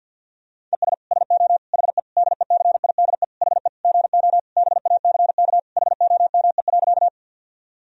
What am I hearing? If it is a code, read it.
Morse code, heard as a keyed tone, about 52 wpm, 700 Hz. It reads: ES SO HE DECIDE HE GO BACK HOME.
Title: Sentences from Top 500 Words and Abbreviations 50wpm